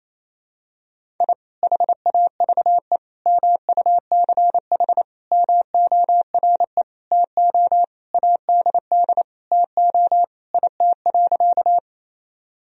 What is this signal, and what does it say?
Morse code, heard as a keyed tone, about 28 wpm, 700 Hz. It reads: I HAVE MUCH MORE TO ADD TO IT.
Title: Sentences from Top 1000 Words - Review Part 1 - 28wpm